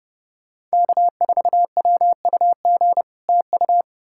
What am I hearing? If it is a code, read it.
Morse code, heard as a keyed tone, about 40 wpm, 700 Hz. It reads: K4WUG TU